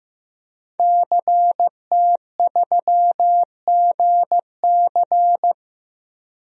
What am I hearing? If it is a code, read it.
Morse code, heard as a keyed tone, about 16 wpm, 700 Hz. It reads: CT3GC
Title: DXCC-100 Call Signs - 5 Characters - Repeated 3x - 15wpm